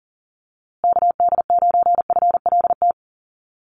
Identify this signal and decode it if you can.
Morse code, heard as a keyed tone, about 40 wpm, 700 Hz. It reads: KD9FLT
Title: USA Call Signs Repeated 3x 40wpm